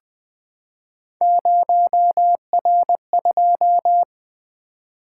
Morse code, heard as a keyed tone, about 20 wpm, 700 Hz. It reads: 0R2
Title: Three Letter-Number Combinations - 20wpm